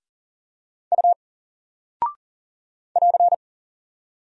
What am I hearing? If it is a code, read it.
Morse code, heard as a keyed tone, about 40 wpm, 700 Hz. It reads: U <AR>
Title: Single Character (Letter:Number:Punctuation:Prosign) 40wpm